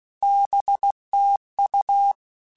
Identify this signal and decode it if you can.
Morse code, heard as a keyed tone, about 16 wpm, 800 Hz. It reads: BTU